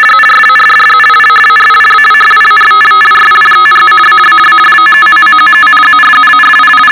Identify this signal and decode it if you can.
Two-tone teleprinter signal, about 450 Hz shift.
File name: Globo Condor LW2DTZ RTTY 50baud 7bit 450Hz